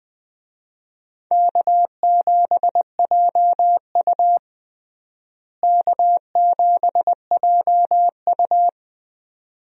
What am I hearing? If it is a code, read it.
Morse code, heard as a keyed tone, about 20 wpm, 700 Hz. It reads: K7JU K7JU
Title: USA Call Signs - 4 Characters - Repeated 2x - 20wpm